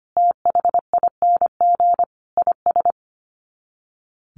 Morse code, heard as a keyed tone, about 25 wpm, 700 Hz. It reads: THING IS
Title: Sets of 5 Words Encore 25wpm